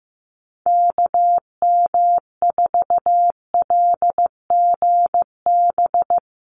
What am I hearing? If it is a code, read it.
Morse code, heard as a keyed tone, about 15 wpm, 700 Hz. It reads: KM4LGB
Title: USA Call Signs 15wpm